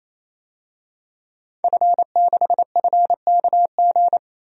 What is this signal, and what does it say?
Morse code, heard as a keyed tone, about 28 wpm, 700 Hz. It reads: F6FKG
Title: DXCC-100 Call Signs - 5 Characters - 28wpm